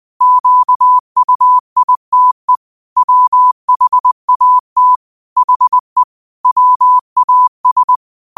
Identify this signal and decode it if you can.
Morse code, heard as a keyed tone, about 20 wpm, 1000 Hz. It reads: QUITE WHAT HE WAS
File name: qotdfiles / 20wpm